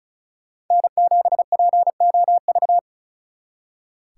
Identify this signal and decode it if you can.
Morse code, heard as a keyed tone, about 46 wpm, 700 Hz. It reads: N7POV